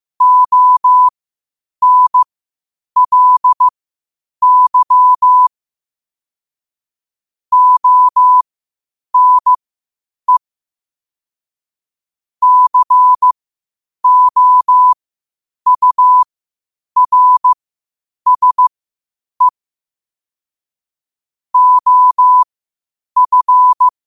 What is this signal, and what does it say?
Morse code, heard as a keyed tone, about 15 wpm, spaced as for 8 wpm, 1000 Hz. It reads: ONLY ONE COURSE OF
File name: qotdfiles / 8wpm